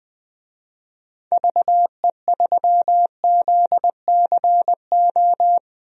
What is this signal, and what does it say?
Morse code, heard as a keyed tone, about 20 wpm, 700 Hz. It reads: VE3ZCO
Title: DXCC-100 Call Signs - 6 Characters - 20wpm